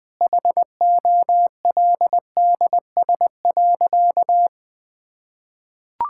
Morse code, heard as a keyed tone, about 20 wpm, 700 Hz. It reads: HOLDS.
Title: Sentences with 5-Word Phrases - Spoken Beginning - 20wpm